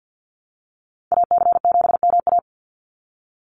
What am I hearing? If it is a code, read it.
Morse code, heard as a keyed tone, about 50 wpm, 700 Hz. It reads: AC7MA
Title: NA SOTA Activator Callsigns 50wpm